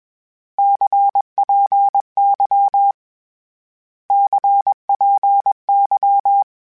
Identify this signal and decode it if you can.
Morse code, heard as a keyed tone, about 20 wpm, 800 Hz. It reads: CPY CPY
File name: Abreviations / Abreviations mp3s / Abreviations@20